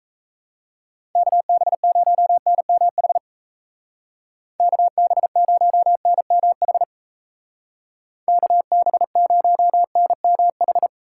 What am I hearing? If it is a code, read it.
Morse code, heard as a keyed tone, about 37 wpm, 700 Hz. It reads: KB0NMH KB0NMH KB0NMH